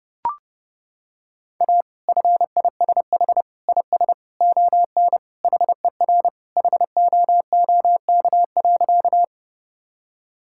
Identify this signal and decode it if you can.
Morse code, heard as a keyed tone, about 32 wpm, 700 Hz. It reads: A FISH IS ON HER HOOK.
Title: Sentences from Top 1600 Words 30wpm